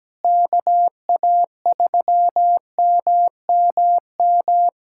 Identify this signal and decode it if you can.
Morse code, heard as a keyed tone, about 17 wpm, 700 Hz. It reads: KA3MMM
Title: USA Callsign DE Callsign 17wpm